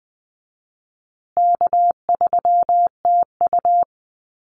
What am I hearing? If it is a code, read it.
Morse code, heard as a keyed tone, about 20 wpm, 700 Hz. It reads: K3TU
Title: USA Call Signs Repeated 3x 20wpm